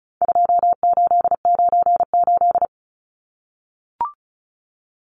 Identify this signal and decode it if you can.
Morse code, heard as a keyed tone, about 34 wpm, 700 Hz. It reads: 2898
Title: Numbers - 4-Digit 35wpm